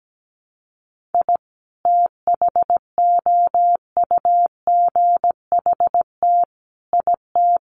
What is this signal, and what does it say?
Morse code, heard as a keyed tone, about 17 wpm, 700 Hz. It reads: I THOUGHT IT
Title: Sets of 5 Words Encore 17wpm